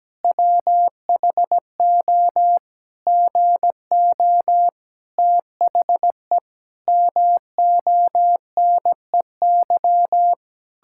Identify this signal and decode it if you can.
Morse code, heard as a keyed tone, about 17 wpm, 700 Hz. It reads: WHO GO THE MONEY
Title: Pangram Challenge v2 - 17wpm